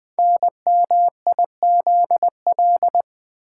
Morse code, heard as a keyed tone, about 20 wpm, 700 Hz. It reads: NMIZL
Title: Code Groups - 5 Letters - 20wpm